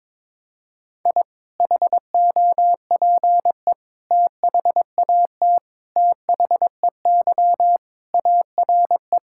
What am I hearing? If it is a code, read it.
Morse code, heard as a keyed tone, about 21 wpm, 700 Hz. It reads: I HOPE THAT THEY ARE
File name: practice / 22wpm / Sentences with 3-Word Phrases - 22wpm